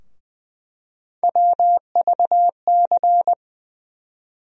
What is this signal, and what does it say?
Morse code, heard as a keyed tone, about 20 wpm, 700 Hz. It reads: WVC
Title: USA Call Sign Postfixes - 3 Letters 20wpm